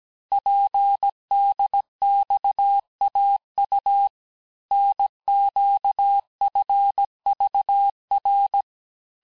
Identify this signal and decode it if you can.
Morse code, heard as a keyed tone, about 17 wpm, 800 Hz. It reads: PDXAU NQFVR
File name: cours-du-lundi files / cours-1587-texte-chiffre-vitesse-16MPM